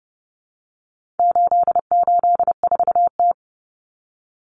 Morse code, heard as a keyed tone, about 30 wpm, 700 Hz. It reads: 884T
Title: Common Cut Numbers - 4-Digit 30wpm